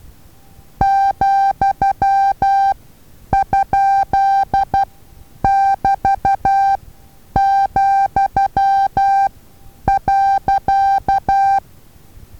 Morse code, heard as a keyed tone, about 12 wpm, spaced as for 9 wpm, 790 Hz. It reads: ,?<BT>,.
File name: Taviro HA6VA A